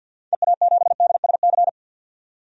Morse code, heard as a keyed tone, about 52 wpm, 700 Hz. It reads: EA7DSC
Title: DXCC-100 Call Signs - 6 Characters - 50wpm